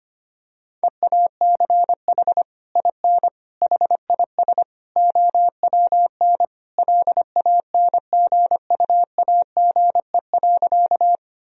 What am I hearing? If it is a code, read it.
Morse code, heard as a keyed tone, about 25 wpm, 700 Hz. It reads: EACH IN HIS OWN LANGUAGE.
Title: Sentences from Top 1000 Words - Review Part 3 - 25wpm